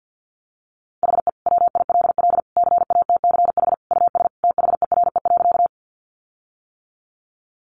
Morse code, heard as a keyed tone, about 50 wpm, 700 Hz. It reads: HE WILL CATCH US THERE.
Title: Sentences from Top 1500 Words 50wpm